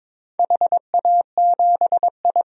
Morse code, heard as a keyed tone, about 22 wpm, 700 Hz. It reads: HA7I
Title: DXCC-100 Call Signs - 4 Characters - 22wpm